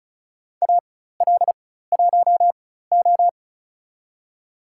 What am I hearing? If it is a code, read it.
Morse code, heard as a keyed tone, about 36 wpm, spaced as for 16 wpm, 700 Hz. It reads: AL1O
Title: USA Call Signs - 4 Characters - 4x Character Spacing - 35wpm